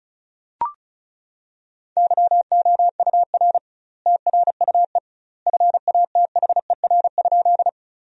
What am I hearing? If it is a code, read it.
Morse code, heard as a keyed tone, about 36 wpm, 700 Hz. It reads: YOUR TRUE FATHER?
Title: 2x Word-Spacing - Sentences from Top 400 Words - 35wpm